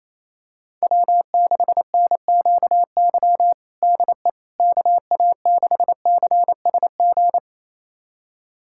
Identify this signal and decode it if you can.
Morse code, heard as a keyed tone, about 28 wpm, 700 Hz. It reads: W6NQY DE KA6CSG
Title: USA Callsign DE Callsign 28wpm